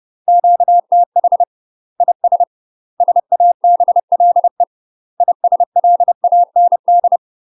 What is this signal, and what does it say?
Morse code, heard as a keyed tone, about 30 wpm, 700 Hz. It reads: QTH IS SABLE ISLAND
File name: UK MASTER QSOs TRAINING / 30WPM / UK MASTER QSOs 0046